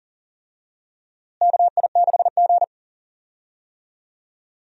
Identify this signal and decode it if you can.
Morse code, heard as a keyed tone, about 40 wpm, 700 Hz. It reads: KI6G
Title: USA Call Signs - 4 Characters - Repeated 3x - 40wpm